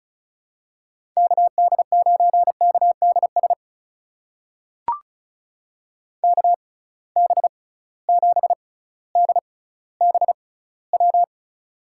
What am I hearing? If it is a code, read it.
Morse code, heard as a keyed tone, about 36 wpm, spaced as for 18 wpm, 690 Hz. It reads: KD9KDS KB7DBW
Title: USA Call Signs - 6 Characters - 6x Character Spacing - 35wpm